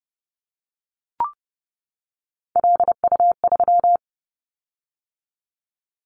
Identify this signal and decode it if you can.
Morse code, heard as a keyed tone, about 30 wpm, 700 Hz. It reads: LU3
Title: Callsign Prefixes for Top 50 DXCC - 30wpm